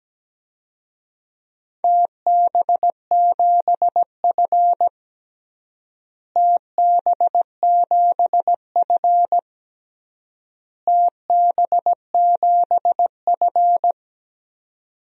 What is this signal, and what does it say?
Morse code, heard as a keyed tone, about 17 wpm, 700 Hz. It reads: TB7F TB7F TB7F
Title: DXCC-100 Call Signs - 4 Characters - Repeated 3x - 17wpm